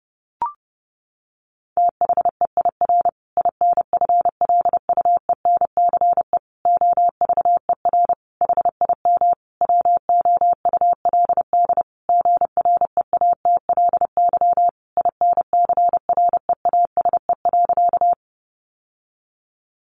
Morse code, heard as a keyed tone, about 30 wpm, 700 Hz. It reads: THEIR INFLUENCE OVER HIM WOULD GREATLY INCREASE.
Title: Sentences from Top 1400 Words 30wpm